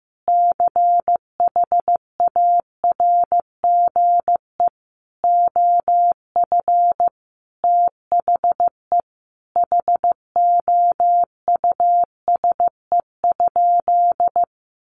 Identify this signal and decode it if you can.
Morse code, heard as a keyed tone, about 15 wpm, 700 Hz. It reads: CHARGE OF THE HOUSE?
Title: Sentences from Top 900 Words 15wpm